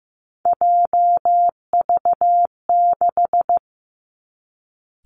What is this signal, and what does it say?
Morse code, heard as a keyed tone, about 15 wpm, 700 Hz. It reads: JV6
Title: Three Challenging Letter-Number Combinations 15wpm